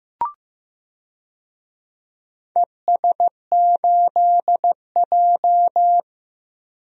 Morse code, heard as a keyed tone, about 15 wpm, 700 Hz. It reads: ES8J
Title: DXCC-100 Call Signs - 4 Characters - Repeated 2x - 15wpm